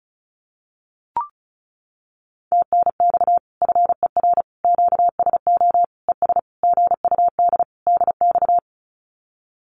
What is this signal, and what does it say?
Morse code, heard as a keyed tone, about 34 wpm, 700 Hz. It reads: TNX FER QSO ES GUD DX
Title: Common QSO Phrases Encore 35wpm